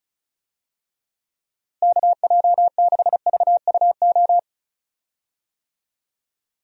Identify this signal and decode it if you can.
Morse code, heard as a keyed tone, about 36 wpm, 700 Hz. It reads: KJ6VUO
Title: USA Call Signs - 6 Characters - Repeated 3x - 35wpm